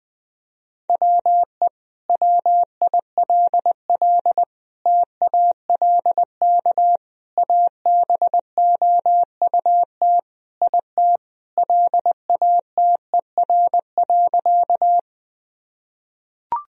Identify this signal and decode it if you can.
Morse code, heard as a keyed tone, about 20 wpm, 700 Hz. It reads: WE WILL TALK ABOUT IT LATER.
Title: Sentences from Top 1000 Words - Review Part 4 - 20wpm